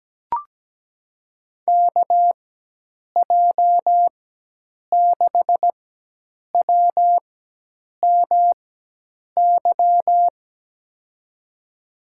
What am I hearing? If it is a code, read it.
Morse code, heard as a keyed tone, about 17 wpm, spaced as for 8 wpm, 700 Hz. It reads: KJ6WMY
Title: USA Call Signs - 6 Characters - 4x Character Spacing - 17wpm